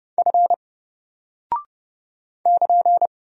Morse code, heard as a keyed tone, about 30 wpm, 700 Hz. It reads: F <KN>
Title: Single Character (Letter:Number:Punctuation:Prosign) Alt 30wpm